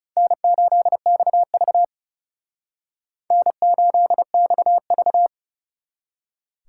Morse code, heard as a keyed tone, about 39 wpm, 700 Hz. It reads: N8XV N8XV